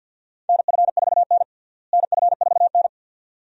Morse code, heard as a keyed tone, about 51 wpm, 700 Hz. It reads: NF4N NF4N
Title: USA Call Signs - 4 Characters - Repeated 2x - 50wpm